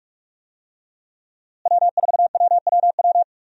Morse code, heard as a keyed tone, about 46 wpm, 700 Hz. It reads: W4WWW
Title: USA Call Signs - 5 Characters - Repeated 2x - 45wpm